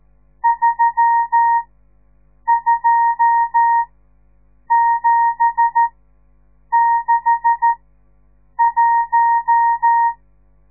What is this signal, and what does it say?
Morse code, heard as a keyed tone, about 13 wpm, spaced as for 7 wpm, 930 Hz. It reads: 32761
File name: radiomorningstar / morse / les 10